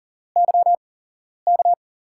Morse code, heard as a keyed tone, about 42 wpm, spaced as for 11 wpm, 700 Hz. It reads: YK
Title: USA Call Signs - 5 Characters - 8x Character Spacing - 40wpm